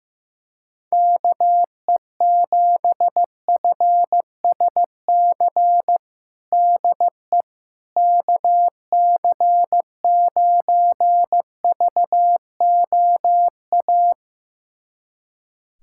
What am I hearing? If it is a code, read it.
Morse code, heard as a keyed tone, about 15 wpm, 700 Hz. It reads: KE7FSC DE KC9VOA
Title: USA Callsign DE Callsign - 6 Characters - 15wpm